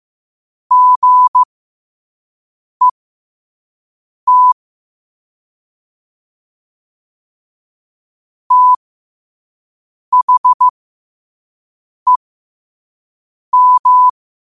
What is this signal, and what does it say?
Morse code, heard as a keyed tone, about 15 wpm, spaced as for 5 wpm, 1000 Hz. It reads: GET THEM
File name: qotdfiles / 5wpm